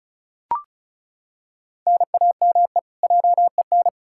Morse code, heard as a keyed tone, about 36 wpm, 700 Hz. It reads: NAME JEN
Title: K1USN SST QSO Practice 35wpm